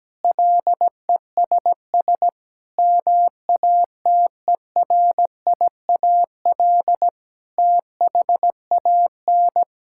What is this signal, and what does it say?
Morse code, heard as a keyed tone, about 17 wpm, 700 Hz. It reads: LESS MATERIAL THAN
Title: Sentences from Top 1000 Words - Review Part 5 - 17wpm